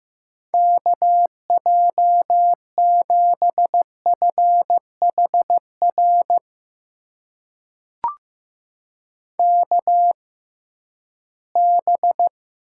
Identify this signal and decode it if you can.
Morse code, heard as a keyed tone, about 15 wpm, spaced as for 11 wpm, 700 Hz. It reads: KJ7FHR KB
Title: USA Call Signs - 6 Characters - 6x Character Spacing - 15wpm